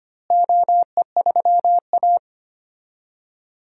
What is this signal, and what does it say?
Morse code, heard as a keyed tone, about 25 wpm, 700 Hz. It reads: OE3A
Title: DXCC-100 Call Signs - 4 Characters - Repeated 3x - 25wpm